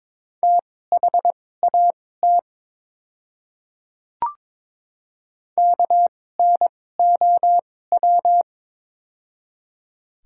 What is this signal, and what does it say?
Morse code, heard as a keyed tone, about 21 wpm, spaced as for 16 wpm, 700 Hz. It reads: THAT KNOW
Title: 2x Character Spacing - Top 100 Words - 22wpm